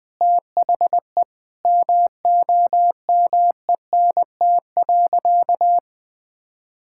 Morse code, heard as a keyed tone, about 21 wpm, 700 Hz. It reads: THE MOMENT.
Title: Sentences from Top 1000 Words - Review Part 6 - 20wpm